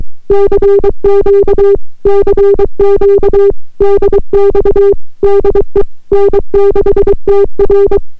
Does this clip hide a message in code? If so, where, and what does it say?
Yes: Morse code from 0.3 to 8.0 s, heard as a keyed tone, about 23 wpm, 400 Hz. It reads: CQ CQ DX DE N6TR